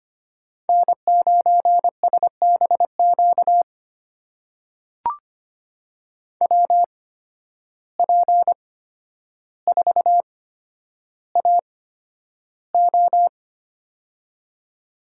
Morse code, heard as a keyed tone, about 25 wpm, spaced as for 11 wpm, 700 Hz. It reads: N9SBQ WP4AO
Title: USA Call Signs - 5 Characters - 8x Character Spacing - 25wpm